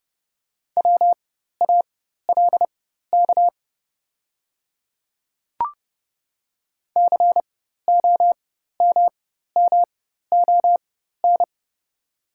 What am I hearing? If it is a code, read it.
Morse code, heard as a keyed tone, about 30 wpm, spaced as for 14 wpm, 700 Hz. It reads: WALK COMMON
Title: 4x Character Spacing - Top 500 Words - 30wpm